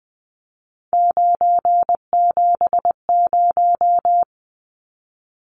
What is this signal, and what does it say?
Morse code, heard as a keyed tone, about 20 wpm, 700 Hz. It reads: 970
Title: Numbers - 3-Digit 20wpm